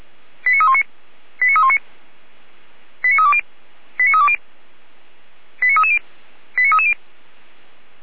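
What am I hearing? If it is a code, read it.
Five-tone selective call (ZVEI), heard as quick runs of five tones, tones of about 70 ms. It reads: 89319 89319 89320 89320 89330 89330
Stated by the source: ZVEI five-tone selective call 89310 received on SDR.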